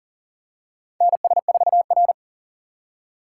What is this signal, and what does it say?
Morse code, heard as a keyed tone, about 40 wpm, 700 Hz. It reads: NS4R
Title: USA Call Signs - 4 Characters - 2x Character Spacing - 40wpm